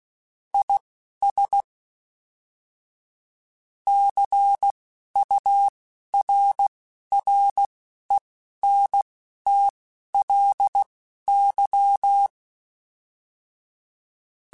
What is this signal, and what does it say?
Morse code, heard as a keyed tone, about 16 wpm, spaced as for 11 wpm, 800 Hz. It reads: IS CURRENTLY
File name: Weather Sets / Weather Sets-mp3s / Weather 5 5 @10